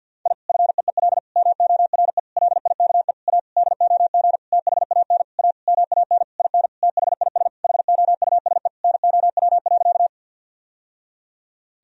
Morse code, heard as a keyed tone, about 51 wpm, 700 Hz. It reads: I FEEL MORE LIKE A DOG THAN A MAN IN THIS HOUSE NOW.
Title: Sentences from Top 1000 Words - Review Part 2 - 50wpm